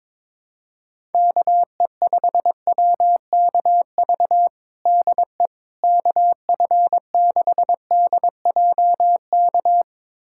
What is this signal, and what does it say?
Morse code, heard as a keyed tone, about 22 wpm, 700 Hz. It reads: KE5WKV DE KF6DJK
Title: USA Callsign DE Callsign - 6 Characters - 22wpm